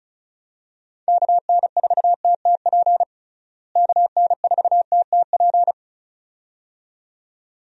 Morse code, heard as a keyed tone, about 34 wpm, 700 Hz. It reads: KN4TTP KN4TTP
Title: USA Call Signs - 6 Characters - Repeated 2x - 35wpm